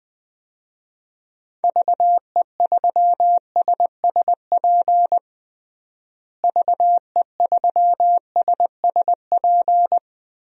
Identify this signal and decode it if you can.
Morse code, heard as a keyed tone, about 20 wpm, 700 Hz. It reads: VE3SSP VE3SSP
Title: DXCC-100 Call Signs - 6 Characters - Repeated 3x - 20wpm